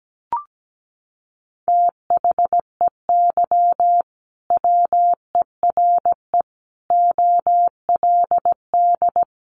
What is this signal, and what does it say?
Morse code, heard as a keyed tone, about 17 wpm, 700 Hz. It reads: THEY WERE OLD
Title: Sentences from Top 800 Words 17wpm